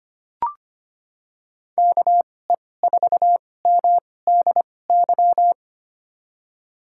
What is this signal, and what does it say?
Morse code, heard as a keyed tone, about 25 wpm, spaced as for 18 wpm, 700 Hz. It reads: KE4MDY
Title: USA Call Signs - 6 Characters - 2x Character Spacing - 25wpm